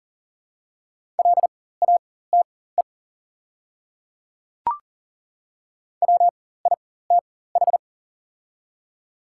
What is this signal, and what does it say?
Morse code, heard as a keyed tone, about 40 wpm, spaced as for 19 wpm, 700 Hz. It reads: LATE WITH
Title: 4x Character Spacing - Top 300 Words - 40wpm